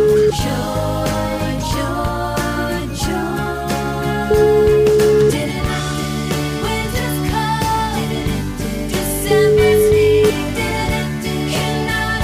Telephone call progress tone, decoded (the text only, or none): ringback tone